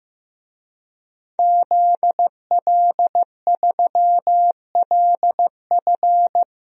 Morse code, heard as a keyed tone, about 15 wpm, 700 Hz. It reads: ZL3LF
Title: DXCC-100 Call Signs - 5 Characters - 15wpm